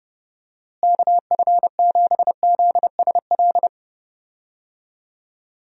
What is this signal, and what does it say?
Morse code, heard as a keyed tone, about 32 wpm, 700 Hz. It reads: KF7ZSL
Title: USA Call Signs - 6 Characters - Repeated 2x - 30wpm